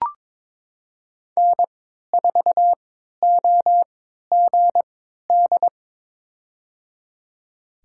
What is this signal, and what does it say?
Morse code, heard as a keyed tone, about 22 wpm, spaced as for 13 wpm, 700 Hz. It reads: N4OGD
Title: USA Call Signs - 5 Characters - 3x Character Spacing - 22wpm